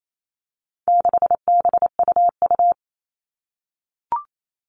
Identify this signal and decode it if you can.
Morse code, heard as a keyed tone, about 28 wpm, 700 Hz. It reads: 6BUU
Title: All Cut Numbers - 4-Digit 28wpm